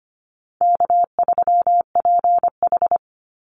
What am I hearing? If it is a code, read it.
Morse code, heard as a keyed tone, about 25 wpm, 700 Hz. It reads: K3PH
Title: Call Signs Repeated 2x - 25wpm